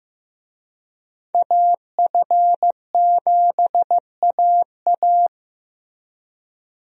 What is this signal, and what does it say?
Morse code, heard as a keyed tone, about 15 wpm, 700 Hz. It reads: AF7AA
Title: USA Call Signs - 5 Characters - Repeated 3x - 15wpm